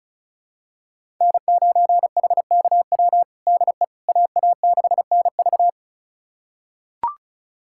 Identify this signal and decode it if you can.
Morse code, heard as a keyed tone, about 34 wpm, 700 Hz. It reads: N9HKW DE AA6NV
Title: USA Callsign DE Callsign - 5 Characters - 35wpm